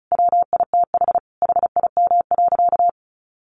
Morse code, heard as a keyed tone, about 34 wpm, 700 Hz. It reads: WITH HIM.
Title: Sentences from Top 900 Words 35wpm